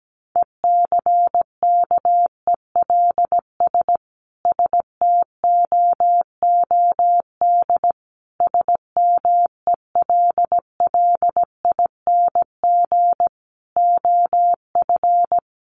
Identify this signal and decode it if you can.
Morse code, heard as a keyed tone, about 17 wpm, 700 Hz. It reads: ECKELS STOOD SMELLING OF
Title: A Sound Of Thunder 17wpm